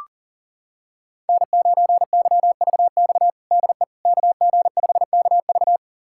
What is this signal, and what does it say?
Morse code, heard as a keyed tone, about 40 wpm, 700 Hz. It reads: N9YVX DE KG5KV
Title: USA Callsign DE Callsign - 5 Characters - 40wpm